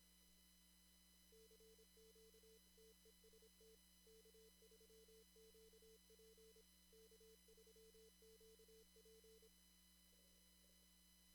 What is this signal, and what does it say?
Morse code, heard as a keyed tone, about 25 wpm, 440 Hz. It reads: CQ TEST K3QP K3QP
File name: CW / kt5j wpx cw-op-k5pi